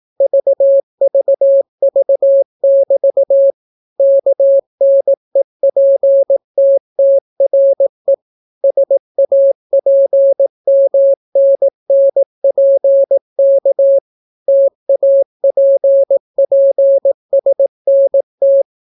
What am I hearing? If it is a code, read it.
Morse code, heard as a keyed tone, about 18 wpm, 550 Hz. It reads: VVV<BT> KNEPTTRE SAPMNNPK TAPPSNT